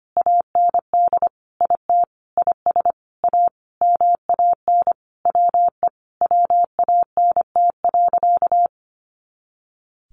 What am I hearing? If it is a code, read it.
Morse code, heard as a keyed tone, about 25 wpm, 700 Hz. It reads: AND IT IS A MAN WE WANT.
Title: Sentences from Top 200 Words 25wpm